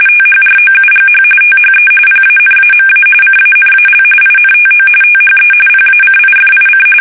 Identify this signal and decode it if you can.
RTTY, about 45.45 baud. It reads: THROUGH THE PANAMA CANAL INTO THE PACIFIC
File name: fsk850 / battleshipnj